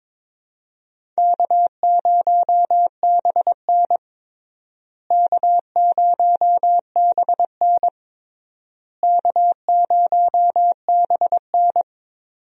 Morse code, heard as a keyed tone, about 22 wpm, 700 Hz. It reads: K0BN K0BN K0BN
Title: USA Call Signs - 4 Characters - Repeated 3x - 22wpm